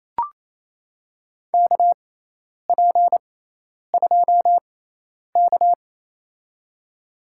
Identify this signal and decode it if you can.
Morse code, heard as a keyed tone, about 28 wpm, spaced as for 10 wpm, 700 Hz. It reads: KP2K
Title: USA Call Signs - 4 Characters - 6x Character Spacing - 28wpm